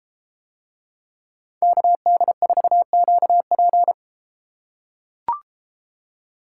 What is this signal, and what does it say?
Morse code, heard as a keyed tone, about 31 wpm, 700 Hz. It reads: KD4QP